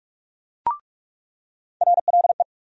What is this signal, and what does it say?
Morse code, heard as a keyed tone, about 43 wpm, 700 Hz. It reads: RLE
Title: USA Call Sign Postfixes - 3 Letters 45wpm